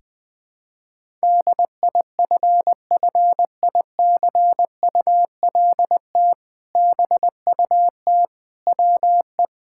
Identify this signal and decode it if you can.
Morse code, heard as a keyed tone, about 20 wpm, 700 Hz. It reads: DIFFICULT BUT WE
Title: Sentences with 2-Word Phrases - Spoken Beginning - 20wpm